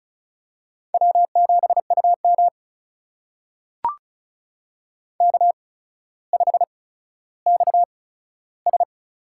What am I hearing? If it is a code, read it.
Morse code, heard as a keyed tone, about 36 wpm, spaced as for 15 wpm, 700 Hz. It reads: W7UM K5XS
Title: USA Call Signs - 4 Characters - 8x Character Spacing - 35wpm